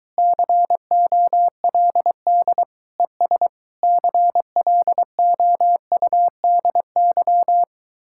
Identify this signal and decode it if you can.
Morse code, heard as a keyed tone, about 22 wpm, 700 Hz. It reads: COLD ES CLOUDY